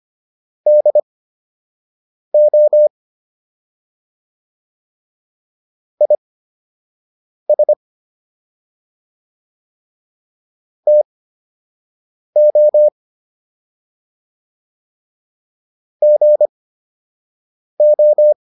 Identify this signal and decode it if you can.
Morse code, heard as a keyed tone, about 25 wpm, spaced as for 6 wpm, 600 Hz. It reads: DO IS TO GO